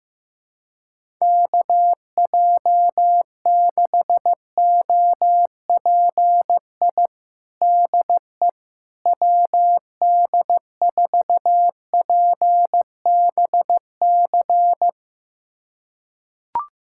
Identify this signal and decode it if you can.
Morse code, heard as a keyed tone, about 15 wpm, 700 Hz. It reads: KJ6OPI DE WD4PBC
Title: USA Callsign DE Callsign - 6 Characters - 15wpm